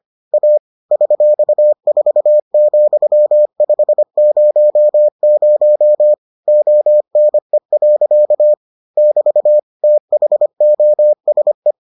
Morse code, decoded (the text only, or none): A $4,500 ONE. <BT> THOSE